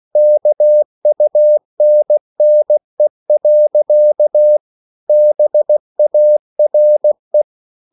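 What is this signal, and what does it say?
Morse code, heard as a keyed tone, about 16 wpm, 600 Hz. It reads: KUNNE. BARE
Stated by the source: Slow (QRS) Morse code at 16 wpm, from CW med Gnister.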